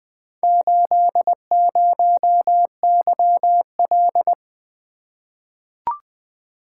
Morse code, heard as a keyed tone, about 20 wpm, 700 Hz. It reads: 80YL
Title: Four Letter-Number Combinations - 20wpm